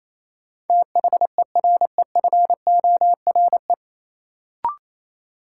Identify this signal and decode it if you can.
Morse code, heard as a keyed tone, about 28 wpm, 700 Hz. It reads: THEREFORE
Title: Sentences from Top 800 Words - New Word Drill - 28wpm